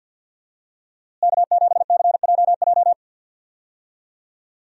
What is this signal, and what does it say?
Morse code, heard as a keyed tone, about 47 wpm, 700 Hz. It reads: K7CJJ
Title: USA Call Signs - 5 Characters - Repeated 3x - 50wpm